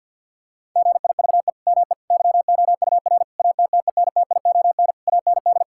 Morse code, heard as a keyed tone, about 50 wpm, 700 Hz. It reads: GIVE ME YOUR ATTENTION AND
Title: Sentences from Top 1000 Words - Review Part 6 - 50wpm